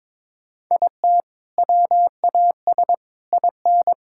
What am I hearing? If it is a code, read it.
Morse code, heard as a keyed tone, about 22 wpm, 700 Hz. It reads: IT WAS IN
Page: morse-code-ninja-practice-22wpm